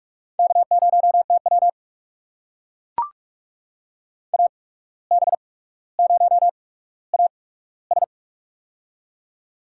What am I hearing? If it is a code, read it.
Morse code, heard as a keyed tone, about 45 wpm, spaced as for 18 wpm, 700 Hz. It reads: K0TW AB0AS
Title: USA Call Signs - 8x Character Spacing - 45wpm